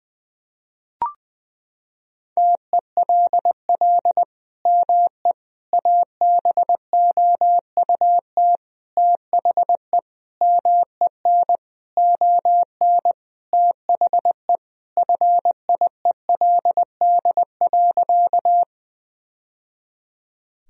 Morse code, heard as a keyed tone, about 21 wpm, 700 Hz. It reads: TELL ME ABOUT THE MEN ON THE FIELD.
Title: Sentences from Top 1000 Words - Review Part 2 - 20wpm